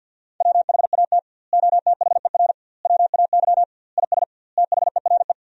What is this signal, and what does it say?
Morse code, heard as a keyed tone, about 50 wpm, 700 Hz. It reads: WHAT OTHER WAY IS THERE
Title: 2x Word-Spacing - Sentences from Top 100 Words - 50wpm